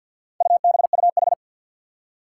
Morse code, heard as a keyed tone, about 53 wpm, 700 Hz. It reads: UBUH
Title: Four Challenging Letter-Number Combinations 50wpm